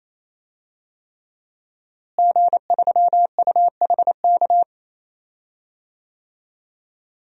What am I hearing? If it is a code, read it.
Morse code, heard as a keyed tone, about 28 wpm, 700 Hz. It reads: G3UHK
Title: DXCC-100 Call Signs - 5 Characters - Repeated 3x - 28wpm